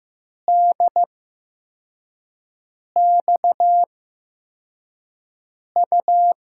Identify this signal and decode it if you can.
Morse code, heard as a keyed tone, about 15 wpm, spaced as for 4 wpm, 700 Hz. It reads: DXU
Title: USA Call Signs - 5 Characters - 8x Character Spacing - 15wpm